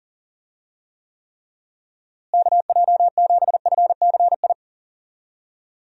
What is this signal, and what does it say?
Morse code, heard as a keyed tone, about 40 wpm, 700 Hz. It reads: KJ7FCI
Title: USA Call Signs - 6 Characters - Repeated 3x - 40wpm